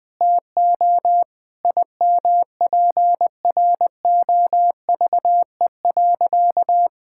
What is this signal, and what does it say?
Morse code, heard as a keyed tone, about 20 wpm, 700 Hz. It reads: TO IMPROVE.
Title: Sentences with 6-Word Phrases - Spoken Beginning - 20wpm